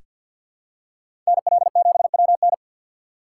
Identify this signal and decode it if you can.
Morse code, heard as a keyed tone, about 47 wpm, 700 Hz. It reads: NL7WN
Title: USA Call Signs - 3x Character Spacing - 50wpm